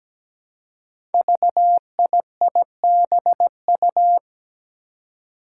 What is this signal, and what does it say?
Morse code, heard as a keyed tone, about 17 wpm, 700 Hz. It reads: VIIBU
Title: Five Challenging Letter-Number Combinations 17wpm